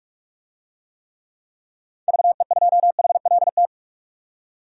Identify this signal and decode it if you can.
Morse code, heard as a keyed tone, about 46 wpm, 700 Hz. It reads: VE2HLT
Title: DXCC-100 Call Signs - 6 Characters - 45wpm